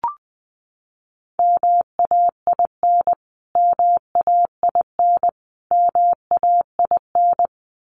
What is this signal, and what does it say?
Morse code, heard as a keyed tone, about 20 wpm, 700 Hz. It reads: MAIN MAIN MAIN
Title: Top 300 Words Repeated 20wpm